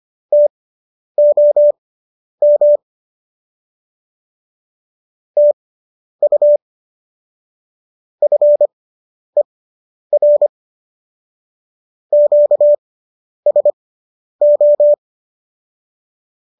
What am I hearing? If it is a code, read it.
Morse code, heard as a keyed tone, about 25 wpm, spaced as for 10 wpm, 590 Hz. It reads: TOM TU FER QSO